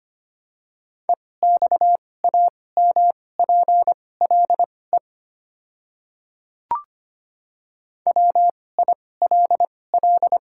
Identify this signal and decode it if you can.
Morse code, heard as a keyed tone, about 25 wpm, spaced as for 18 wpm, 700 Hz. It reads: EXAMPLE WILL
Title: 2x Character Spacing - Top 300 Words - 25wpm